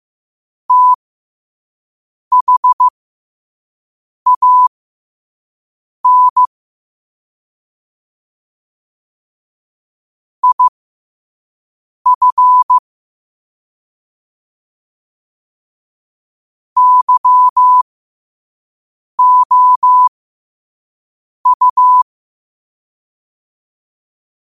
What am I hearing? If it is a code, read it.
Morse code, heard as a keyed tone, about 15 wpm, spaced as for 5 wpm, 1000 Hz. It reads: THAN IF YOU
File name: qotdfiles / 5wpm